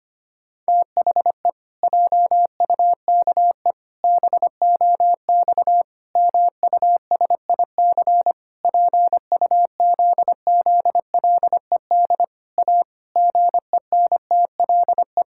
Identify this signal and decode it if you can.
Morse code, heard as a keyed tone, about 25 wpm, 700 Hz. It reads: THE JUKE BOX MUSIC PUZZLED A GENTLE
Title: Pangram Challenge v2 - 25wpm